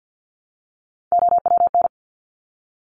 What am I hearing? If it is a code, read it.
Morse code, heard as a keyed tone, about 50 wpm, 700 Hz. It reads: OWN
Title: Top 500 Words Repeated 50wpm